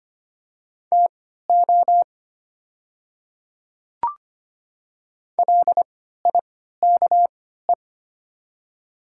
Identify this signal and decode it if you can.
Morse code, heard as a keyed tone, about 25 wpm, spaced as for 14 wpm, 690 Hz. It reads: TO LIKE
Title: 3x Character Spacing - Top 200 Words - 25wpm